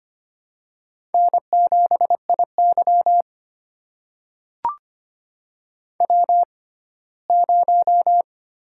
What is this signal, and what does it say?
Morse code, heard as a keyed tone, about 25 wpm, spaced as for 17 wpm, 700 Hz. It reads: N7IY W0
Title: USA Call Signs - 4 Characters - 6x Character Spacing - 25wpm